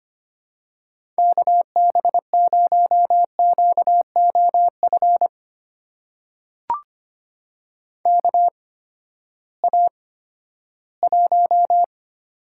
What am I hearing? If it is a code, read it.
Morse code, heard as a keyed tone, about 25 wpm, spaced as for 14 wpm, 700 Hz. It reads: KB0QOF KA1
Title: USA Call Signs - 6 Characters - 8x Character Spacing - 25wpm